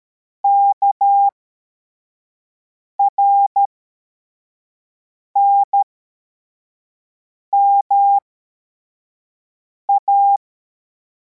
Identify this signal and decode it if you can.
Morse code, heard as a keyed tone, about 13 wpm, spaced as for 4 wpm, 800 Hz. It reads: KRNMA
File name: formation / cw / Text23